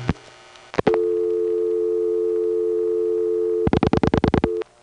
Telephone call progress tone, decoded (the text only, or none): dial tone